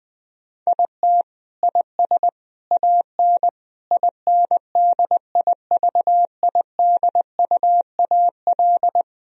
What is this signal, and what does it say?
Morse code, heard as a keyed tone, about 21 wpm, 700 Hz. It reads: IT IS AN INDIVIDUAL
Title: Sentences from Top 1000 Words - Review Part 4 - 20wpm